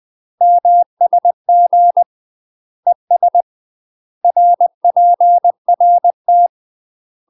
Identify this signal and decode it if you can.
Morse code, heard as a keyed tone, about 20 wpm, 700 Hz. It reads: MSG ES RPRT
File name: UK MASTER QSOs TRAINING / 20WPM / UK MASTER QSOs 0052